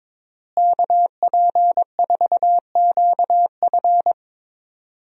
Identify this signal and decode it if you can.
Morse code, heard as a keyed tone, about 22 wpm, 700 Hz. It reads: KP4QF
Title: DXCC-100 Call Signs - 5 Characters - Repeated 2x - 22wpm